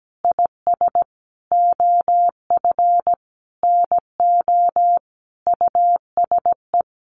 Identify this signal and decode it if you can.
Morse code, heard as a keyed tone, about 17 wpm, 700 Hz. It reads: IS OF NO USE
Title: Sentences from Top 1400 Words 17wpm